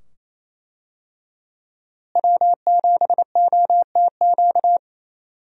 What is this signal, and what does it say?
Morse code, heard as a keyed tone, about 28 wpm, 700 Hz. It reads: W7OTQ
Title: USA Call Signs - 5 Characters - Repeated 2x - 28wpm